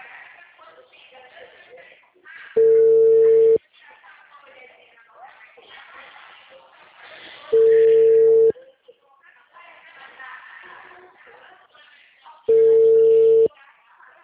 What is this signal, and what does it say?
Ringback tone (China, 450 Hz), with speech.